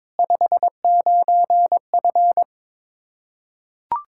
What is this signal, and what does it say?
Morse code, heard as a keyed tone, about 22 wpm, 700 Hz. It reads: 59F
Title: Practice QSO - Exchange 2 - Info - 22wpm